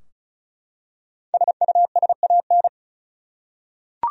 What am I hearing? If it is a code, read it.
Morse code, heard as a keyed tone, about 36 wpm, 700 Hz. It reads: SUSAN
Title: USA Common Names - 35wpm